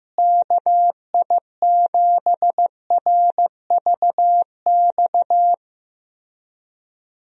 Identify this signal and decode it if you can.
Morse code, heard as a keyed tone, about 15 wpm, 700 Hz. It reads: KI7RVX
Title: USA Call Signs - 6 Characters - Repeated 2x - 15wpm